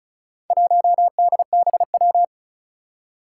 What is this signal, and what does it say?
Morse code, heard as a keyed tone, about 36 wpm, 700 Hz. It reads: 1DBW
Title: Four Challenging Letter-Number Combinations 35wpm